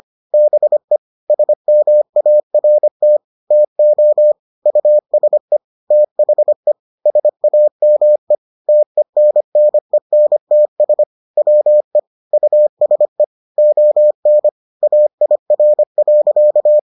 Morse code, heard as a keyed tone, about 25 wpm, 600 Hz. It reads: BE SMART TO USE THE SAME TENNENTS WE USE ON AIR.